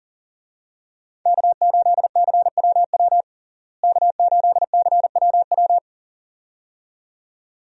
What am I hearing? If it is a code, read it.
Morse code, heard as a keyed tone, about 40 wpm, 700 Hz. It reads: K8CWW K8CWW
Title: USA Call Signs - 5 Characters - Repeated 2x - 40wpm